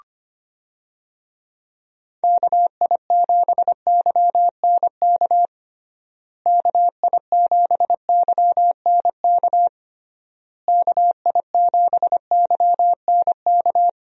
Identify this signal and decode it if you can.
Morse code, heard as a keyed tone, about 25 wpm, 700 Hz. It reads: KI7YNK KI7YNK KI7YNK
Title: USA Call Signs - 6 Characters - Repeated 3x - 25wpm